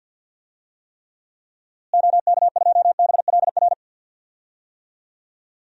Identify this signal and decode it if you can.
Morse code, heard as a keyed tone, about 50 wpm, 700 Hz. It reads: OK2BLR
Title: DXCC-100 Call Signs - 6 Characters - Repeated 3x - 50wpm